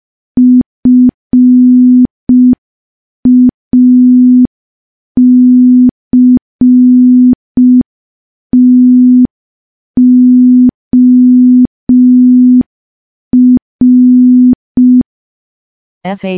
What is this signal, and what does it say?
Morse code, heard as a keyed tone, about 5 wpm, 250 Hz. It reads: FACTOR
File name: morse / morse-training-05WPM